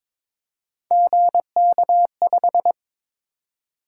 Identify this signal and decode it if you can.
Morse code, heard as a keyed tone, about 22 wpm, 700 Hz. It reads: GK5
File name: practice / 22wpm / Three Letter-Number Combinations - 22wpm